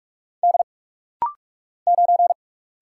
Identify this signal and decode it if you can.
Morse code, heard as a keyed tone, about 44 wpm, 700 Hz. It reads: D 9
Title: Single Letter-Number - Rapid-Fire 45wpm